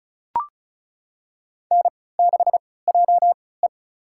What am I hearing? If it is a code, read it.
Morse code, heard as a keyed tone, about 36 wpm, spaced as for 20 wpm, 700 Hz. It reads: N6JE
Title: USA Call Signs - 4 Characters - 3x Character Spacing - 35wpm